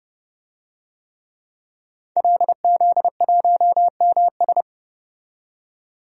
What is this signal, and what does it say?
Morse code, heard as a keyed tone, about 31 wpm, 700 Hz. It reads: LZ1MS
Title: DXCC-100 Call Signs - 5 Characters - 30wpm